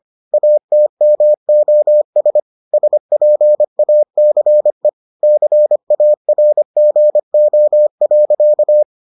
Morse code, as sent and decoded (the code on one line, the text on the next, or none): .- - -- --- ... / ... .--. .- -.-. / -.-. .- .-. --. --- .-.-.-
ATMOS SPACE CARGO.